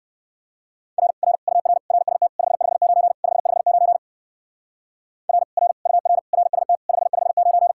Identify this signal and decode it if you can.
Morse code, heard as a keyed tone, about 59 wpm, 700 Hz. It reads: R R UR RST 559 559 R R UR RST 559